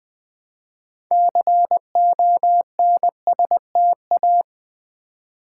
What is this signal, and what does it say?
Morse code, heard as a keyed tone, about 20 wpm, 700 Hz. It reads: CONSTA
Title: Build A Word - 2 Letters At A Time - 8-Letter Words - 20wpm